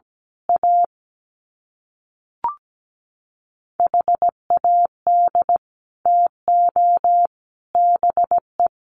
Morse code, heard as a keyed tone, about 17 wpm, 700 Hz. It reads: A HAD TO BE
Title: Sets of 3 Words 17wpm